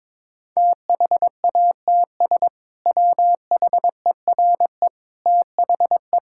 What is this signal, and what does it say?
Morse code, heard as a keyed tone, about 21 wpm, 700 Hz. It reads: THATS WHERE THE
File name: practice / 22wpm / Sentences with 2-Word Phrases - 22wpm